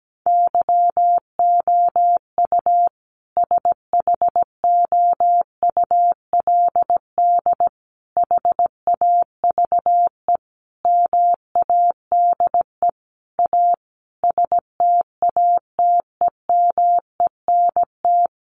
Morse code, heard as a keyed tone, about 17 wpm, 700 Hz. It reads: YOU SHOULD HAVE MADE A STATEMENT
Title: Sentences from Top 1200 Words 17wpm